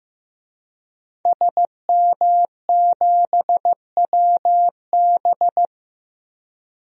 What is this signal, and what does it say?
Morse code, heard as a keyed tone, about 15 wpm, 700 Hz. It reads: SM7WB
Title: DXCC-100 Call Signs - 5 Characters - Repeated 2x - 15wpm